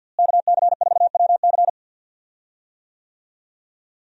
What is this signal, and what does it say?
Morse code, heard as a keyed tone, about 52 wpm, 700 Hz. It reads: KC4WC
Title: USA Call Signs - 5 Characters - Repeated 3x - 50wpm